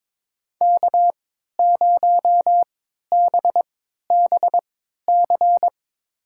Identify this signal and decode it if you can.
Morse code, heard as a keyed tone, about 22 wpm, spaced as for 13 wpm, 700 Hz. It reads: K0BBC
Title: USA Call Signs - 5 Characters - 3x Character Spacing - 22wpm